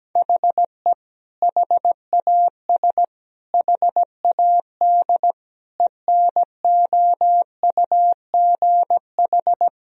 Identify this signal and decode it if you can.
Morse code, heard as a keyed tone, about 17 wpm, 700 Hz. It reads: HE HAS HAD ENOUGH
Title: Sentences from Top 1000 Words - Review Part 3 - 17wpm